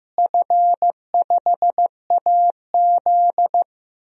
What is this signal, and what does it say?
Morse code, heard as a keyed tone, about 15 wpm, 700 Hz. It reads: F5AZ
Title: DXCC-100 Call Signs - 4 Characters - Repeated 2x - 15wpm